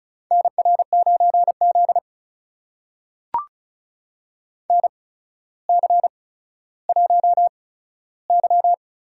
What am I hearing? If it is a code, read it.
Morse code, heard as a keyed tone, about 36 wpm, spaced as for 15 wpm, 700 Hz. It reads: NR9Z NC1Y
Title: USA Call Signs - 4 Characters - 8x Character Spacing - 35wpm